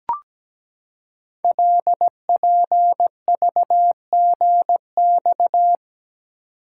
Morse code, heard as a keyed tone, about 17 wpm, 700 Hz. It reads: LPVGX
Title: Code Groups - 5 Letters - 17wpm